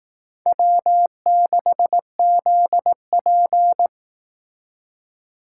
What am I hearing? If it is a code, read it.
Morse code, heard as a keyed tone, about 18 wpm, 700 Hz. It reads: W6ZP
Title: Speed Racing - USA Call Signs 15wpm